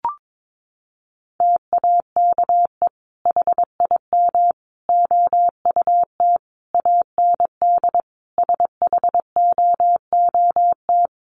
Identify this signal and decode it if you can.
Morse code, heard as a keyed tone, about 22 wpm, 700 Hz. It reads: TAKE HIM OUT AND SHOOT
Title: Sentences from Top 1200 Words 22wpm